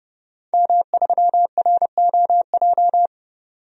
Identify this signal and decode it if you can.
Morse code, heard as a keyed tone, about 31 wpm, 700 Hz. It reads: M3ROJ
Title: DXCC-100 Call Signs - 5 Characters - 30wpm